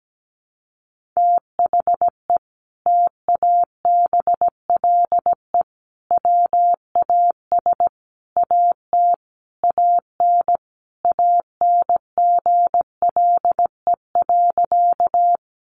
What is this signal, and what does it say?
Morse code, heard as a keyed tone, about 17 wpm, 700 Hz. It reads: THE TABLE WAS AT AN ANGLE.
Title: Sentences from Top 1500 Words 17wpm